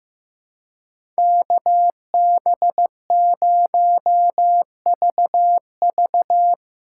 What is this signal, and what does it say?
Morse code, heard as a keyed tone, about 15 wpm, 700 Hz. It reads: KB0VV
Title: USA Call Signs - 4x Character Spacing - 15wpm